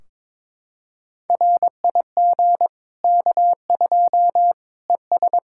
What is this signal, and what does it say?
Morse code, heard as a keyed tone, about 21 wpm, 700 Hz. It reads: RIG K2 ES
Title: K1USN SST QSO Practice 22wpm